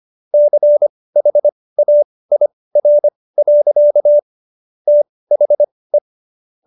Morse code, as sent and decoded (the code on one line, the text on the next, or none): -.-. .... .- .. .-. .-.-.- / - .... .
CHAIR. THE